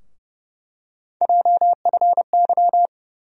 Morse code, heard as a keyed tone, about 30 wpm, 700 Hz. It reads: JFY
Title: USA Call Sign Postfixes - 3 Letters 30wpm